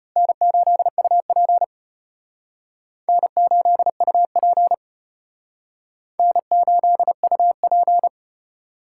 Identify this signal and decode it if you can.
Morse code, heard as a keyed tone, about 31 wpm, 700 Hz. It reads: N8UP N8UP N8UP
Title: Speed Racing - USA Call Signs 25wpm